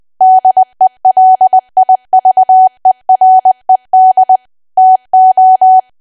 Morse code, heard as a keyed tone, about 20 wpm, 750 Hz. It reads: DELIVERED TO
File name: cwa / W1AW-Level3 / 20WPM